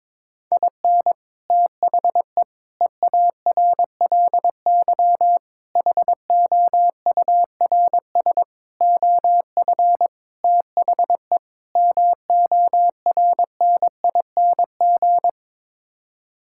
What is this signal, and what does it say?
Morse code, heard as a keyed tone, about 22 wpm, 700 Hz. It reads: IN THE EARLY HOURS OF THE MORNING
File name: practice / 22wpm / Sets of 7 Words - 22wpm